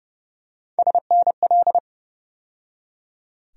Morse code, heard as a keyed tone, about 30 wpm, 700 Hz. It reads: SNL
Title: USA Call Sign Postfixes - 3 Letters 30wpm